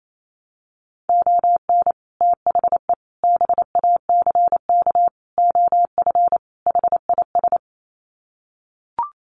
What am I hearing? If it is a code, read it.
Morse code, heard as a keyed tone, about 28 wpm, 700 Hz. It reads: ON THE BACK OF HIS
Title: Sets of 5 Words Encore 28wpm